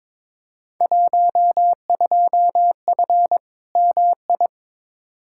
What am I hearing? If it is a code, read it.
Morse code, heard as a keyed tone, about 22 wpm, 700 Hz. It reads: 12F MI
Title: Field Day Exchanges Encore - 22wpm